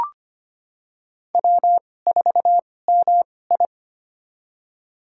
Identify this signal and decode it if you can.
Morse code, heard as a keyed tone, about 25 wpm, spaced as for 18 wpm, 700 Hz. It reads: W4MI
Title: USA Call Signs - 4 Characters - 2x Character Spacing - 25wpm